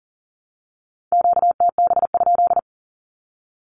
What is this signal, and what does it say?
Morse code, heard as a keyed tone, about 40 wpm, 700 Hz. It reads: QTB?
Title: Q-Codes 40wpm